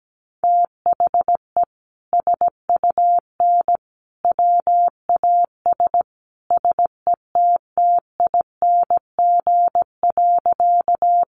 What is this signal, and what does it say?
Morse code, heard as a keyed tone, about 17 wpm, 700 Hz. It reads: THE SUN WAS SETTING.
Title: Sentences from Top 1300 Words 17wpm